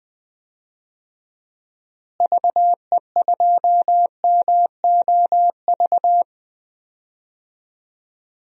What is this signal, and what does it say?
Morse code, heard as a keyed tone, about 21 wpm, 700 Hz. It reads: VE2MOV
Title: DXCC-100 Call Signs - 6 Characters - Repeated 3x - 20wpm